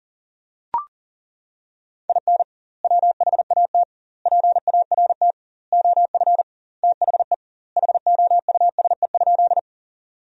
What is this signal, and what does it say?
Morse code, heard as a keyed tone, about 40 wpm, 700 Hz. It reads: IN WHAT PART OF THE HOUSE?
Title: 2x Word-Spacing - Sentences from Top 200 Words - 40wpm